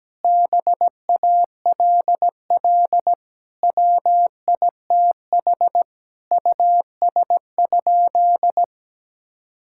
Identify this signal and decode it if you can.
Morse code, heard as a keyed tone, about 17 wpm, 700 Hz. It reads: BALL WITH US?
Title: Sentences from Top 1000 Words - Review Part 6 - 17wpm